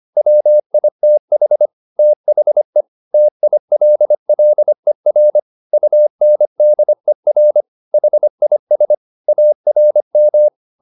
Morse code, heard as a keyed tone, about 25 wpm, 600 Hz. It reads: WITH THE TILLER UNDER HIS ARM